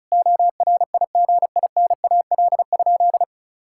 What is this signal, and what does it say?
Morse code, heard as a keyed tone, about 36 wpm, 700 Hz. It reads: ORIGINAL?
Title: Sentences from Top 1000 Words - Review Part 6 - 35wpm